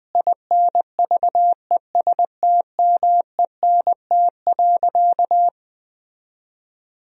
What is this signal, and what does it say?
Morse code, heard as a keyed tone, about 20 wpm, 700 Hz. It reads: INVESTMENT.
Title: Sentences from Top 1000 Words - Review Part 4 - 20wpm